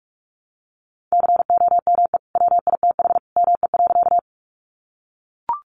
Morse code, heard as a keyed tone, about 44 wpm, 700 Hz. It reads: COME WITH ME.
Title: Sentences from Top 200 Words 45wpm